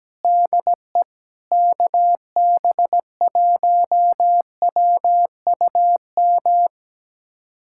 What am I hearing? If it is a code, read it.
Morse code, heard as a keyed tone, about 17 wpm, 700 Hz. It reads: DE KB1WUM
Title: USA Callsign DE Callsign 17wpm